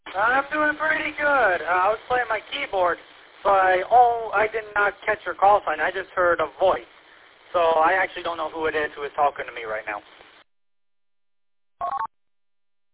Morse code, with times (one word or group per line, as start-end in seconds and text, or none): none